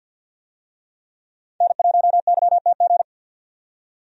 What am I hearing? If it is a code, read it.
Morse code, heard as a keyed tone, about 49 wpm, 700 Hz. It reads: N1YTG
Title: USA Call Signs - 5 Characters - 50wpm